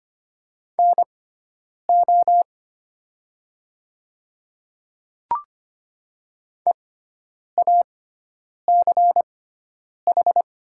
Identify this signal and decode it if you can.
Morse code, heard as a keyed tone, about 25 wpm, spaced as for 9 wpm, 700 Hz. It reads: NO EACH
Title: 6x Character Spacing - Top 100 Words - 25wpm